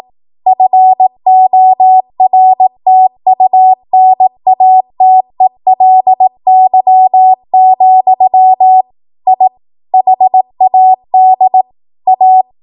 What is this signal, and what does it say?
Morse code, heard as a keyed tone, about 18 wpm, 750 Hz. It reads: FORTUNATELY, I HAD A
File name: Morse / 18WPM